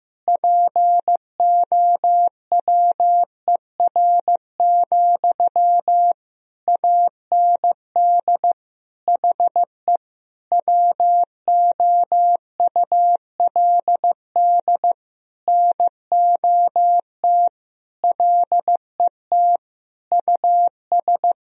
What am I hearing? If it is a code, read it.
Morse code, heard as a keyed tone, about 16 wpm, 700 Hz. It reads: POWER, AND HE WOULD NOT LET US
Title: Sentences from Top 1000 Words - Review Part 2 - 15wpm